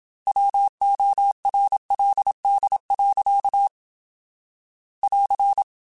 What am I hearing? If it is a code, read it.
Morse code, heard as a keyed tone, about 27 wpm, 800 Hz. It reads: WORLD. <AR>
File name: Quotes / Quotes-mp3s / ChildAsleep @25